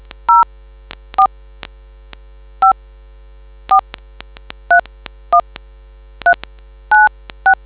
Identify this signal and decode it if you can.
DTMF, heard as telephone touch tones, tones of about 110 ms. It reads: 04 5 4 31 396